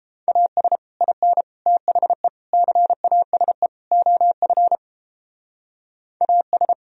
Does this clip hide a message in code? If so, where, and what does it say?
Yes: Morse code from 0.3 to 6.7 s, heard as a keyed tone, about 31 wpm, 700 Hz. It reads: AS IN THE CASE OF AS